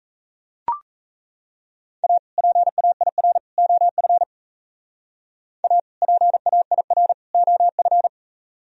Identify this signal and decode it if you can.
Morse code, heard as a keyed tone, about 34 wpm, 700 Hz. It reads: A PAIR OF A PAIR OF
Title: Speed-Racing - Sets of 3 Words 28wpm